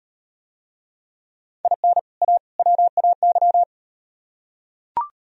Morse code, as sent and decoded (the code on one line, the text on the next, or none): .. -. / .- / .-- .- -.--
IN A WAY